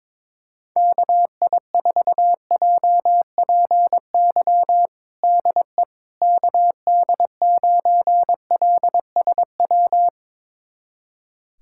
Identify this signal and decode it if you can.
Morse code, heard as a keyed tone, about 22 wpm, 700 Hz. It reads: KI4JPY DE KD9LSW
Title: USA Callsign DE Callsign 22wpm